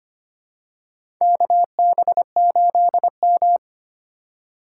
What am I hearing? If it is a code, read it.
Morse code, heard as a keyed tone, about 25 wpm, 700 Hz. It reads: KB8M
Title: K1USN SST QSO Practice - 25wpm